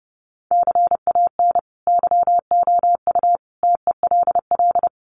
Morse code, heard as a keyed tone, about 30 wpm, 700 Hz. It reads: CAN YOU TELL
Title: Sets of 5 Words Encore2 30wpm